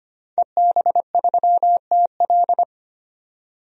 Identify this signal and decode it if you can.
Morse code, heard as a keyed tone, about 25 wpm, 700 Hz. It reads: EB3TL
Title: DXCC-100 Call Signs - 5 Characters - 25wpm